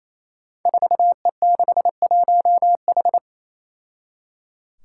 Morse code, heard as a keyed tone, about 27 wpm, 700 Hz. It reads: 4E61H
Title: Five Challenging Letter-Number Combinations 28wpm